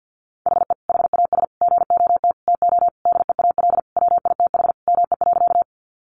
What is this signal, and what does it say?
Morse code, heard as a keyed tone, about 50 wpm, 700 Hz. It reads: HE HAS GOT TO DEAL WITH ME.
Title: Sentences from Top 800 Words 50wpm